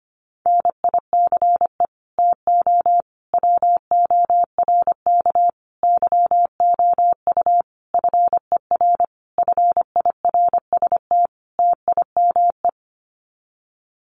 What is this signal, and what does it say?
Morse code, heard as a keyed tone, about 25 wpm, 700 Hz. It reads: NICE TO WORK YOU FER FIRST TIME
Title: Common QSO Phrases 25wpm